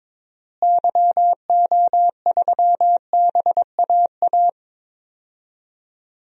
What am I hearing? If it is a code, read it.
Morse code, heard as a keyed tone, about 21 wpm, 700 Hz. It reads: YO3BAA
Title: DXCC-100 Call Signs - 6 Characters - Repeated 3x - 22wpm